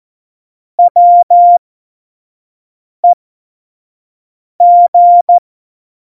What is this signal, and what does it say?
Morse code, heard as a keyed tone, about 14 wpm, spaced as for 5 wpm, 700 Hz. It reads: WEG
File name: Morsekurs / Witze / Witz 29. Dezember 2020, 25 F70